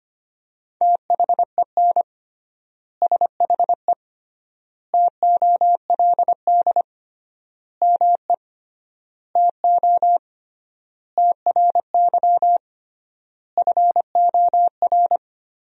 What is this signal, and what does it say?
Morse code, heard as a keyed tone, about 25 wpm, 700 Hz. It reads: THEN SHE TOLD ME TO TRY FOR
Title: 3x Word-Spacing - Sentences from Top 400 Words - 25wpm